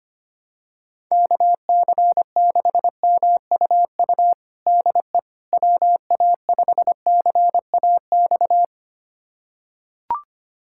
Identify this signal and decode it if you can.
Morse code, heard as a keyed tone, about 25 wpm, 700 Hz. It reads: KC6MUU DE WA5CAX
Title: USA Callsign DE Callsign - 6 Characters - 25wpm